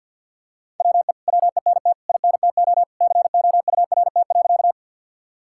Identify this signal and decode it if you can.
Morse code, heard as a keyed tone, about 50 wpm, 700 Hz. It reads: WE WENT INTO COURT.
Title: Sentences from Top 1600 Words 50wpm